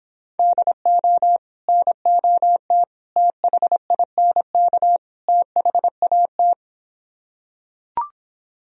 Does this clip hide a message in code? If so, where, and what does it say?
Yes: Morse code from 0.4 to 6.5 s, heard as a keyed tone, about 26 wpm, 700 Hz. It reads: DO NOT THINK THAT